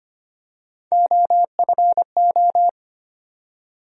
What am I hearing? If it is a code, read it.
Morse code, heard as a keyed tone, about 25 wpm, 700 Hz. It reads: OFO
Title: USA Call Sign Postfixes - 3 Letters 25wpm